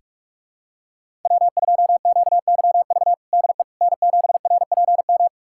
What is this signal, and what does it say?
Morse code, heard as a keyed tone, about 45 wpm, 700 Hz. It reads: W2QYV DE N7RPM
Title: USA Callsign DE Callsign - 5 Characters - 45wpm